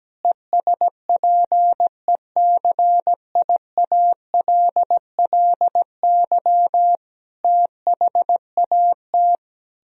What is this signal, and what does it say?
Morse code, heard as a keyed tone, about 17 wpm, 700 Hz. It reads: ESPECIALLY THAT
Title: Sentences from Top 1000 Words - Review Part 4 - 17wpm